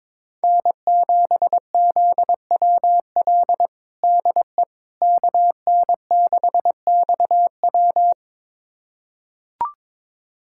Morse code, heard as a keyed tone, about 22 wpm, 700 Hz. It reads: N7ZWL DE KN6XW
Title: USA Callsign DE Callsign - 5 Characters - 22wpm